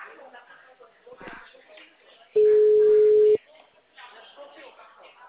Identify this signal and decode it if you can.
Ringback tone (Europe, 425 Hz), with speech.
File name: out- -2007- - - .49726